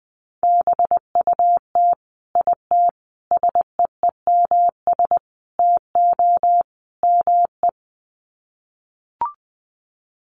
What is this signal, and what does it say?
Morse code, heard as a keyed tone, about 20 wpm, 700 Hz. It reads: BUT IT SEEMS TO ME
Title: Sets of 5 Words Encore2 20wpm